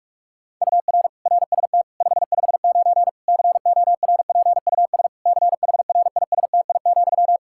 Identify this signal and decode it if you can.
Morse code, heard as a keyed tone, about 41 wpm, 700 Hz. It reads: UR RST 559 CORPUS CHRISTI,
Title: Practice QSO - Exchange 1 - Beginning - 45wpm